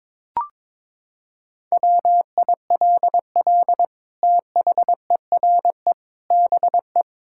Morse code, heard as a keyed tone, about 22 wpm, 700 Hz. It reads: WILL THERE BE
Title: Sentences from Top 1600 Words 22wpm